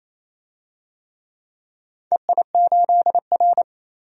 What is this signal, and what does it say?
Morse code, heard as a keyed tone, about 28 wpm, 700 Hz. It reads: EI8R
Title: DXCC-100 Call Signs - 4 Characters - 28wpm